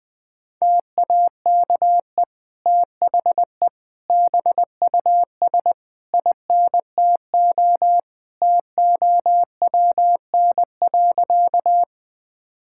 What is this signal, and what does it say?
Morse code, heard as a keyed tone, about 20 wpm, 700 Hz. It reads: TAKE THE BUS INTO TOWN.
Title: Sentences from Top 1000 Words - Review Part 6 - 20wpm